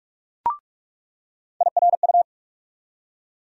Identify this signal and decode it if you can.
Morse code, heard as a keyed tone, about 42 wpm, 700 Hz. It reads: IRU
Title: USA Call Sign Postfixes - 3 Letters 45wpm